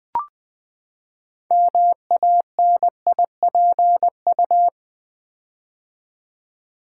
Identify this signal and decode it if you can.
Morse code, heard as a keyed tone, about 20 wpm, 700 Hz. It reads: MANIPU
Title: Build A Word - 6 Letters At A Time - 12-Letter Words - 20wpm